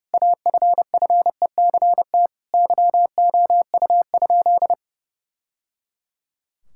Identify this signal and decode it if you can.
Morse code, heard as a keyed tone, about 30 wpm, 700 Hz. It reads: AFFECT YOU?
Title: Sentences from Top 1000 Words - Review Part 6 - 30wpm